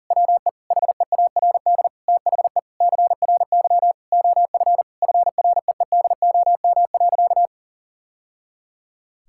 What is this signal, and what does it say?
Morse code, heard as a keyed tone, about 40 wpm, 700 Hz. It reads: WE HEARD THE CRY OF FREEDOM.
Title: Sentences from Top 1000 Words - Review Part 5 - 40wpm